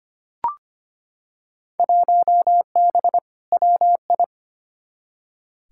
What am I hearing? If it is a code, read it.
Morse code, heard as a keyed tone, about 25 wpm, 700 Hz. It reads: 1B WI
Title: Field Day Exchanges Encore - 25wpm